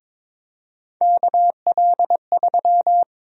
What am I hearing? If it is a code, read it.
Morse code, heard as a keyed tone, about 21 wpm, 700 Hz. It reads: KL3
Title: USA Call Sign Prefixes - 22wpm